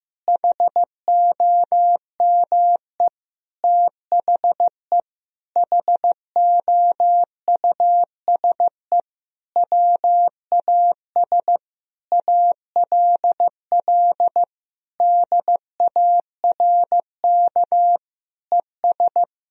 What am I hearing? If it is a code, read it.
Morse code, heard as a keyed tone, about 15 wpm, 700 Hz. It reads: HOME THE HOUSE WAS ALL DARK ES
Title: Sentences from Top 500 Words and Abbreviations 15wpm